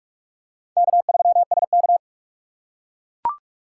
Morse code, heard as a keyed tone, about 41 wpm, 700 Hz. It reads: K3SK
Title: USA Call Signs - 4 Characters - 6x Character Spacing - 45wpm